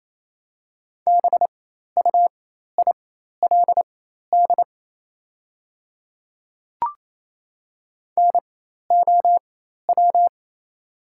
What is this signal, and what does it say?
Morse code, heard as a keyed tone, about 27 wpm, spaced as for 13 wpm, 690 Hz. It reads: BUILD NOW
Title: 4x Character Spacing - Top 300 Words - 28wpm